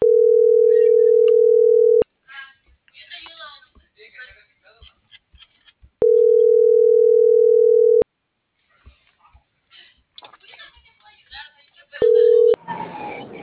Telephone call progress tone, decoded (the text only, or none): ringback tone